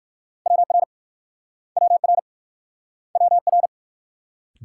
Morse code, heard as a keyed tone, about 54 wpm, 690 Hz. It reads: WR WR WR
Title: Speed-Racing Multi-letter Phonemes - 45wpm